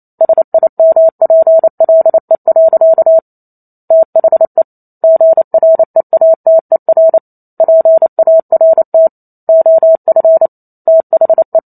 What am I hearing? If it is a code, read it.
Morse code, heard as a keyed tone, about 28 wpm, 650 Hz. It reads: SIMPLE. THE GREATER PART OF THE